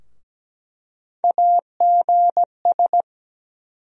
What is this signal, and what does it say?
Morse code, heard as a keyed tone, about 17 wpm, 700 Hz. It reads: AGS
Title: USA Call Sign Postfixes - 3 Letters 17wpm